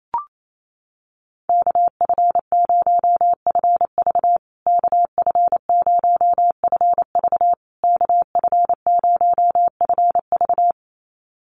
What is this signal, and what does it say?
Morse code, heard as a keyed tone, about 28 wpm, 700 Hz. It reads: KF0FV KF0FV KF0FV
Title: USA Call Signs Repeated 3x 28wpm